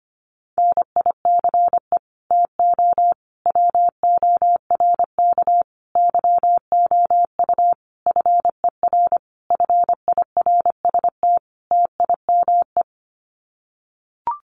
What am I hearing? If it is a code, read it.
Morse code, heard as a keyed tone, about 25 wpm, 700 Hz. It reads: NICE TO WORK YOU FER FIRST TIME
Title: Common QSO Phrases 25wpm